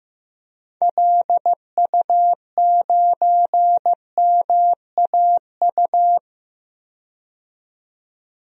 Morse code, heard as a keyed tone, about 16 wpm, 700 Hz. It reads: LU9MAU
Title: DXCC-100 Call Signs - 6 Characters - 15wpm